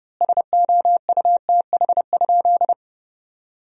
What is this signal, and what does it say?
Morse code, heard as a keyed tone, about 30 wpm, 700 Hz. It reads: SOUTH?
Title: 2x Word-Spacing - Sentences from Top 400 Words - 30wpm